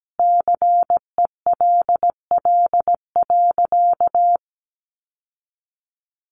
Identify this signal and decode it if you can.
Morse code, heard as a keyed tone, about 17 wpm, 700 Hz. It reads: CELL.
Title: Sentences from Top 1100 Words 17wpm